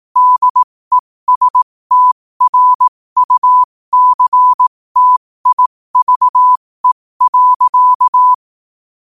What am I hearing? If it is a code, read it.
Morse code, heard as a keyed tone, about 18 wpm, spaced as for 15 wpm, 1000 Hz. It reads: DESTRUCTIVE.
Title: 15 WPM morse code quotes for Fri, 01 Mar 2024